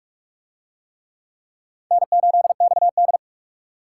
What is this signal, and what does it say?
Morse code, heard as a keyed tone, about 44 wpm, 700 Hz. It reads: N8XD